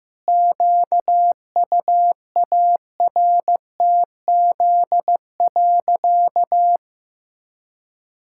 Morse code, heard as a keyed tone, about 15 wpm, 700 Hz. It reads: QUARTZ.
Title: Pangram Challenge v2 - 15wpm